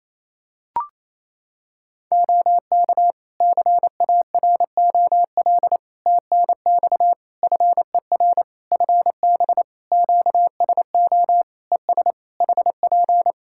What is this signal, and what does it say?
Morse code, heard as a keyed tone, about 28 wpm, 700 Hz. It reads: OK CAROL TNX FER FB QSO ES HP
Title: K1USN SST QSO Practice 28wpm